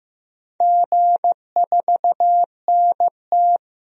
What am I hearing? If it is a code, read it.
Morse code, heard as a keyed tone, about 15 wpm, 700 Hz. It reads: G4NT
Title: DXCC-100 Call Signs - 4 Characters - Repeated 2x - 15wpm